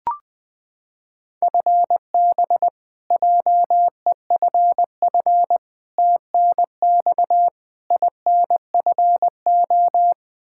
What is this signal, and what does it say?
Morse code, heard as a keyed tone, about 21 wpm, 700 Hz. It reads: FB JEFF TNX INFO
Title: Practice QSO - Exchange 3 - More Info - 20wpm